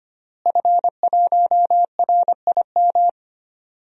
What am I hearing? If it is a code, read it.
Morse code, heard as a keyed tone, about 25 wpm, 700 Hz. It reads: F1RIM
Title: DXCC-100 Call Signs - 5 Characters - Repeated 2x - 25wpm